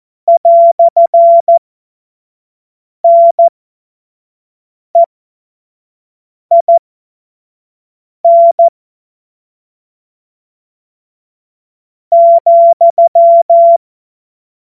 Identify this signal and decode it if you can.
Morse code, heard as a keyed tone, about 14 wpm, spaced as for 5 wpm, 680 Hz. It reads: "NEIN ,
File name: Witz-05-August-2025-25-F70